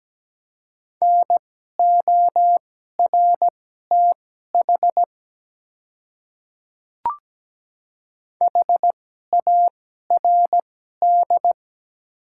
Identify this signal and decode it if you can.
Morse code, heard as a keyed tone, about 17 wpm, spaced as for 12 wpm, 700 Hz. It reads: NORTH HARD
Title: 2x Character Spacing - Top 300 Words - 17wpm